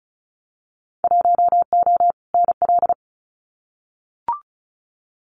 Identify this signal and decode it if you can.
Morse code, heard as a keyed tone, about 36 wpm, 700 Hz. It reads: 1O NL
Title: Winter Field Day Exchanges Encore 35wpm